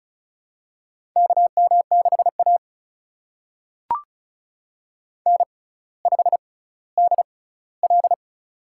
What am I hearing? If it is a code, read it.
Morse code, heard as a keyed tone, about 36 wpm, spaced as for 18 wpm, 700 Hz. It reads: KM6A N5DL
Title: USA Call Signs - 4 Characters - 6x Character Spacing - 35wpm